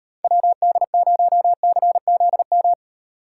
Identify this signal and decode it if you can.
Morse code, heard as a keyed tone, about 38 wpm, 700 Hz. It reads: WD0CZM